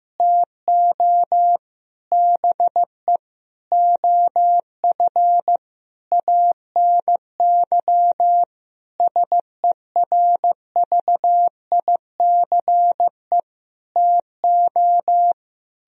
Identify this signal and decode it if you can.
Morse code, heard as a keyed tone, about 16 wpm, 700 Hz. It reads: TO BE OF ANY SERVICE TO
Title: Sentences from Top 1000 Words - Review Part 4 - 15wpm